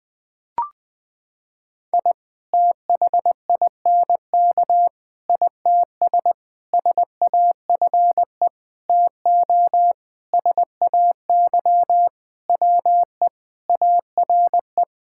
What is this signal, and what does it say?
Morse code, heard as a keyed tone, about 21 wpm, 700 Hz. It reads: I THINK ITS SAFE TO SAY WE ARE
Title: Sentences with 6-Word Phrases - 20wpm